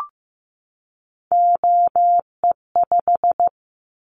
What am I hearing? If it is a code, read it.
Morse code, heard as a keyed tone, about 15 wpm, 700 Hz. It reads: OE5
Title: Callsign Prefixes for Top 100 DXCC - 15wpm